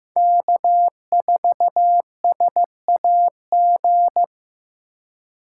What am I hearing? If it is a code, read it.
Morse code, heard as a keyed tone, about 15 wpm, 700 Hz. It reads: K4SAG
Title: USA Call Signs - 4x Character Spacing - 15wpm